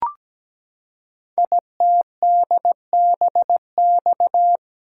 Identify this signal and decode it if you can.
Morse code, heard as a keyed tone, about 17 wpm, 700 Hz. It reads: ITDBX
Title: Code Groups - 5 Letters - 17wpm